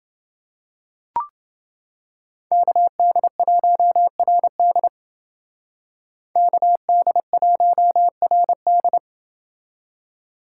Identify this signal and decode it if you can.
Morse code, heard as a keyed tone, about 29 wpm, 700 Hz. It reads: KD1RD KD1RD